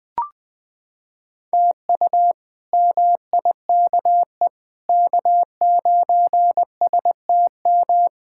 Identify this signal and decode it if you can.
Morse code, heard as a keyed tone, about 20 wpm, 700 Hz. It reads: TU MIKE K9STM
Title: K1USN SST QSO Practice - 20wpm